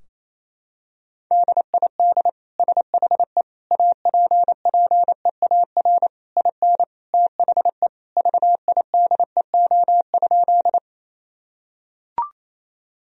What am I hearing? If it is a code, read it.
Morse code, heard as a keyed tone, about 28 wpm, 700 Hz. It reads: DID SHE APPEAR IN THE VIDEO?
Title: Sentences from Top 1000 Words - Review Part 4 - 28wpm